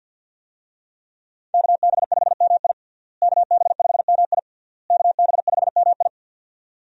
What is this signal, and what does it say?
Morse code, heard as a keyed tone, about 50 wpm, 700 Hz. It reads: KB5MI KB5MI KB5MI
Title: USA Call Signs - 5 Characters - Repeated 3x - 50wpm